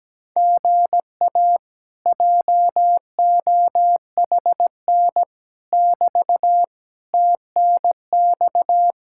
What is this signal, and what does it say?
Morse code, heard as a keyed tone, about 17 wpm, 700 Hz. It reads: GA JOHN <BT> TNX
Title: Speed-Racing Common QSO Phrases Encore - 17wpm